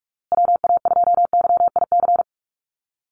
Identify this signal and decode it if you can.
Morse code, heard as a keyed tone, about 44 wpm, 700 Hz. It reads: WA2YIC
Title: USA Call Signs Repeated 3x 45wpm